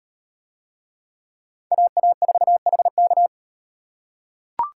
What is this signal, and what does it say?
Morse code, heard as a keyed tone, about 37 wpm, 700 Hz. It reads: AA4HK